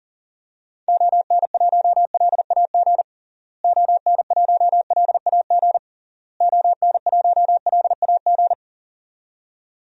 Morse code, heard as a keyed tone, about 40 wpm, 700 Hz. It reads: ON1LAG ON1LAG ON1LAG
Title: DXCC-100 Call Signs - 6 Characters - Repeated 3x - 40wpm